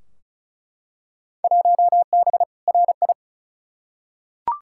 Morse code, heard as a keyed tone, about 36 wpm, 700 Hz. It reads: 1B RI
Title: Field Day Exchanges Encore - 35wpm